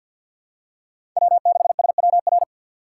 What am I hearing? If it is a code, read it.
Morse code, heard as a keyed tone, about 48 wpm, 700 Hz. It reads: W6SWR
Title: USA Call Signs - 5 Characters - 4x Character Spacing - 50wpm